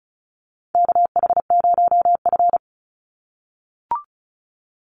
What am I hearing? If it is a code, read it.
Morse code, heard as a keyed tone, about 34 wpm, 700 Hz. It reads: KH0F
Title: USA Call Signs 35wpm